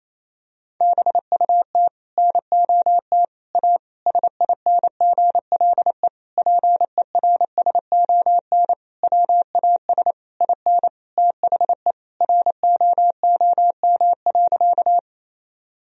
Morse code, heard as a keyed tone, about 27 wpm, 700 Hz. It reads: BUT NOT A SINGLE PERSON WAS IN THE ROOM.
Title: Sentences from Top 1000 Words - Review Part 4 - 28wpm